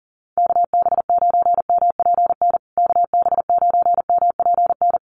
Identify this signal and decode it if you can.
Morse code, heard as a keyed tone, about 40 wpm, 700 Hz. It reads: KB9MPN KB9MPN
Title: USA Call Signs Repeated 3x 40wpm